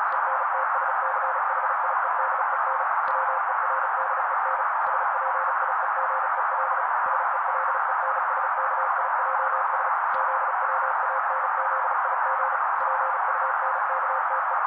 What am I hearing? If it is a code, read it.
Morse code, heard as a keyed tone, about 35 wpm, 530 Hz. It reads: ENTSCHIEDEN WERDEN SPIEGEL SELENSKIJS ZWANZIG PUNKTE
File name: daily / 35wpm noise